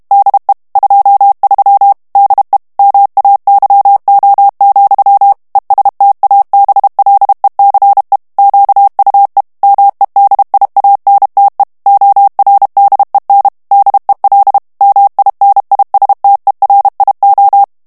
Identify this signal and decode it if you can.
Morse code, heard as a keyed tone, about 32 wpm, 800 Hz. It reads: DE 23 DE MAYO, ESTABLECE QUE MEDIANTE ORDEN DEL MINISTERIO
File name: ESPAGNOL TEXTE EN CLAIR 2 VITESSE 30 MOTS MINUTES